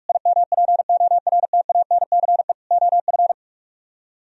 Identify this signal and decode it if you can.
Morse code, heard as a keyed tone, about 45 wpm, 690 Hz. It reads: IMPORTANCE OF